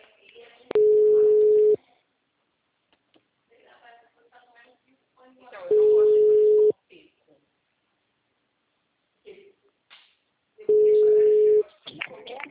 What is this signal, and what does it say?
Ringback tone (Europe, 425 Hz), with speech.
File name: out- -2028- - - .21159